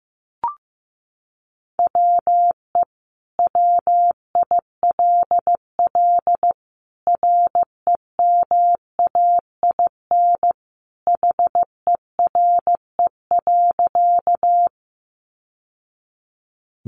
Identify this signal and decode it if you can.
Morse code, heard as a keyed tone, about 15 wpm, 700 Hz. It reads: WE WILL REMAIN HERE.
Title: Sentences from Top 1200 Words 15wpm